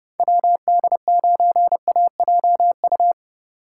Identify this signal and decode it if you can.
Morse code, heard as a keyed tone, about 29 wpm, 700 Hz. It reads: WD9AJU